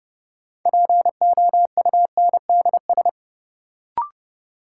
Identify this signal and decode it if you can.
Morse code, heard as a keyed tone, about 30 wpm, 700 Hz. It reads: POUNDS
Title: Sentences from Top 1000 Words - Review Part 2 - New Word Drill - 30wpm